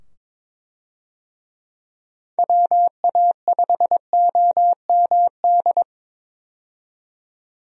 Morse code, heard as a keyed tone, about 22 wpm, 700 Hz. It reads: WA5OMD
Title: USA Call Signs - 6 Characters - 22wpm